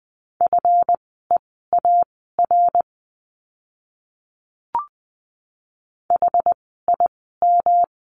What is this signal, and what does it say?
Morse code, heard as a keyed tone, about 20 wpm, spaced as for 15 wpm, 700 Hz. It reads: FEAR HIM
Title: 139 Lesson 23f - Letter f - Words 20wpm